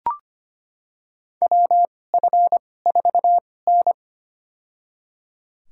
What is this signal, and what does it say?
Morse code, heard as a keyed tone, about 25 wpm, spaced as for 18 wpm, 700 Hz. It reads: WF4N
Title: USA Call Signs - 4 Characters - 2x Character Spacing - 25wpm